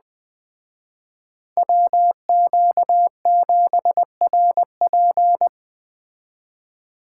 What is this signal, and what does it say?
Morse code, heard as a keyed tone, about 20 wpm, 700 Hz. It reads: WQ7RP
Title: Speed Racing - USA 2x2 Call Signs 17wpm